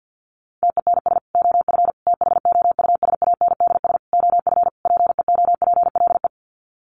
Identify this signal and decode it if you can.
Morse code, heard as a keyed tone, about 50 wpm, 700 Hz. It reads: TENS OF THOUSANDS OF PEOPLE
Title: Sets of 5 Words Encore2 50wpm